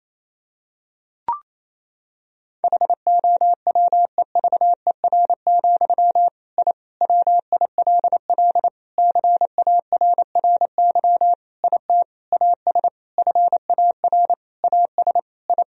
Morse code, heard as a keyed tone, about 27 wpm, 700 Hz. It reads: HOWEVER, I WILL CARRY IT AS FAR AS I
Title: Sentences from Top 1000 Words - Review Part 1 - 28wpm